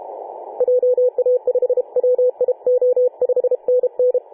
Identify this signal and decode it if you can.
Morse code, heard as a keyed tone, about 32 wpm, 500 Hz, with radio noise. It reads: JA5WIO 5NN
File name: 2013/Sep/21 2035z 3D2GC 18.072MHz CW